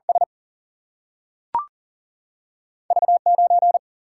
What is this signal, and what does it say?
Morse code, heard as a keyed tone, about 40 wpm, 700 Hz. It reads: S V 9
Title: Single-Letter Single-Number 40wpm